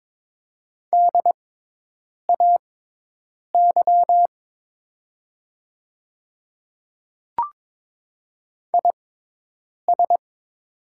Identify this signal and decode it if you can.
Morse code, heard as a keyed tone, about 22 wpm, spaced as for 8 wpm, 700 Hz. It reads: DAY IS
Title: 6x Character Spacing - Top 200 Words - 22wpm